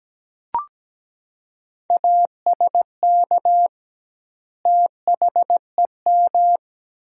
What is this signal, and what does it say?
Morse code, heard as a keyed tone, about 17 wpm, 700 Hz. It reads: ASK THEM
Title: 2x Word-Spacing - Sentences from Top 400 Words - 17wpm